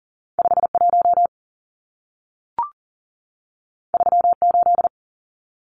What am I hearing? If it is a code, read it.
Morse code, heard as a keyed tone, about 40 wpm, 700 Hz. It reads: 51 38
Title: Numbers - 2-Digit 40wpm